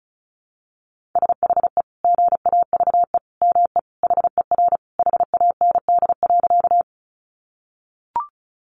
Morse code, heard as a keyed tone, about 34 wpm, 700 Hz. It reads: SHE GAVE ME HER HAND.
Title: Sentences from Top 500 Words 35wpm